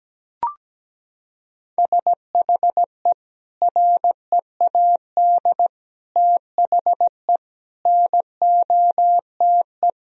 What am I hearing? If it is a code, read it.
Morse code, heard as a keyed tone, about 17 wpm, 700 Hz. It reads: SHE READ THE NOTE
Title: Sentences from Top 1000 Words - Review Part 2 - 17wpm